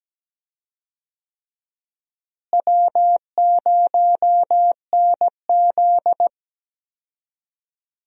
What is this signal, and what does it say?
Morse code, heard as a keyed tone, about 17 wpm, 700 Hz. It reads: W0NZ
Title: USA Call Signs - 4 Characters - Repeated 3x - 17wpm